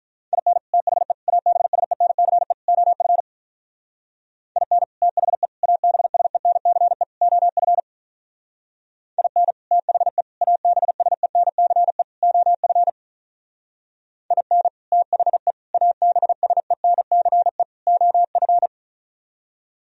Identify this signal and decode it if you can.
Morse code, heard as a keyed tone, about 46 wpm, 700 Hz. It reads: IN THE ABSENCE OF IN THE ABSENCE OF IN THE ABSENCE OF IN THE ABSENCE OF